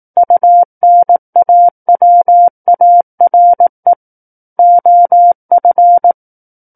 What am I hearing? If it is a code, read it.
Morse code, heard as a keyed tone, about 18 wpm, 700 Hz. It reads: UNAWARE OF
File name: 18 WPM